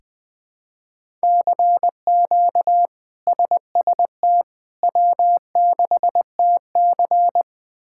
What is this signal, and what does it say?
Morse code, heard as a keyed tone, about 20 wpm, 700 Hz. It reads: CQ SST W6TC
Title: K1USN SST QSO Practice - 20wpm